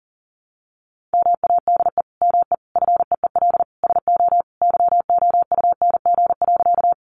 Morse code, heard as a keyed tone, about 40 wpm, 700 Hz. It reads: MADE ME FEEL SO YOUNG.
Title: Sentences from Top 400 Words 40wpm